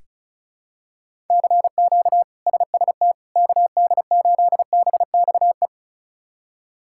Morse code, heard as a keyed tone, about 34 wpm, 700 Hz. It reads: CQ SST KD8BXE
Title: K1USN SST QSO Practice - 35wpm